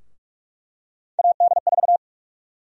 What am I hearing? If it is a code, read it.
Morse code, heard as a keyed tone, about 46 wpm, 700 Hz. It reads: AD4
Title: USA Call Sign Prefixes - Repeated 2x - 45wpm